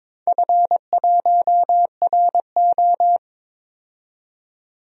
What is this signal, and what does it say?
Morse code, heard as a keyed tone, about 22 wpm, 700 Hz. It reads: F1RO
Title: DXCC-100 Call Signs - 4 Characters - 22wpm